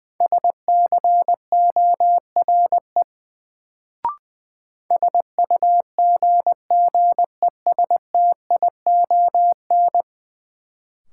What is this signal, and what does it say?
Morse code, heard as a keyed tone, about 20 wpm, 700 Hz. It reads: SCORE SUGGESTION
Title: Sentences from Top 1600 Words - New Word Drill - 20wpm